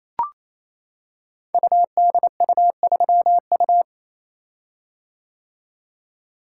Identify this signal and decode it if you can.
Morse code, heard as a keyed tone, about 27 wpm, 700 Hz. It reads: UDU3U
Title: Code Groups - 5 Letter:Numbers - 28wpm